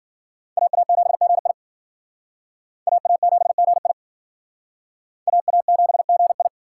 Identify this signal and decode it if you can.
Morse code, heard as a keyed tone, about 52 wpm, 700 Hz. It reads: AA7GI AA7GI AA7GI